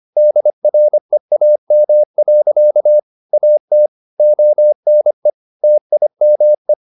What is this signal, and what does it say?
Morse code, heard as a keyed tone, about 25 wpm, 600 Hz. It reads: DREAM. AT ONE TIME